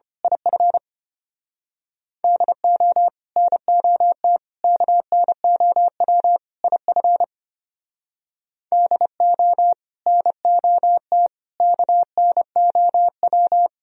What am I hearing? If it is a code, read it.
Morse code, heard as a keyed tone, about 30 wpm, 700 Hz. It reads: IF DO NOT KNOW IF DO NOT KNOW
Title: Speed-Racing - Sets of 4 Words 25wpm